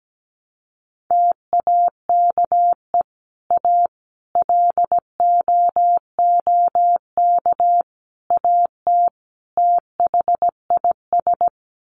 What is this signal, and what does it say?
Morse code, heard as a keyed tone, about 17 wpm, 700 Hz. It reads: TAKE A LOOK AT THIS
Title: Sets of 5 Words 17wpm